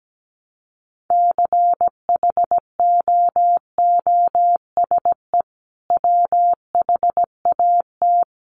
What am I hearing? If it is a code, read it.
Morse code, heard as a keyed tone, about 17 wpm, 700 Hz. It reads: CHOOSE WHAT
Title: Sentences from Top 700 Words 17wpm